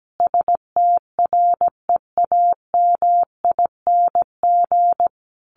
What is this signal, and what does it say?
Morse code, heard as a keyed tone, about 17 wpm, 700 Hz. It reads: STREAMING
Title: A Sound Of Thunder 17wpm